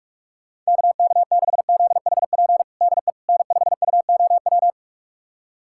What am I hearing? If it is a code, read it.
Morse code, heard as a keyed tone, about 42 wpm, 700 Hz. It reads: KK6ZHP DE N5UOW
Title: USA Callsign DE Callsign 45wpm